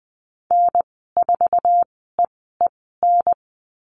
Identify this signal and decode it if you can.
Morse code, heard as a keyed tone, about 20 wpm, spaced as for 14 wpm, 700 Hz. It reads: N4EEN
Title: 140 Lesson 23g - Letter f - Call Signs 20wpm